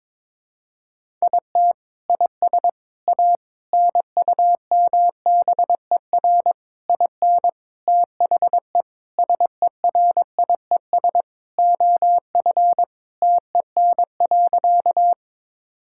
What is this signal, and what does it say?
Morse code, heard as a keyed tone, about 22 wpm, 700 Hz. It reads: IT IS A NUMBER IN THE SERIES OF TEN.
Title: Sentences from Top 1000 Words - Review Part 5 - 22wpm